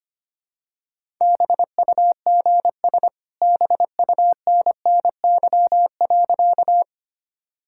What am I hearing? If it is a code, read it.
Morse code, heard as a keyed tone, about 25 wpm, 700 Hz. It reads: BUGS BUNNY.
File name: practice / 25wpm / All-Age Appropriate Jokes - 25wpm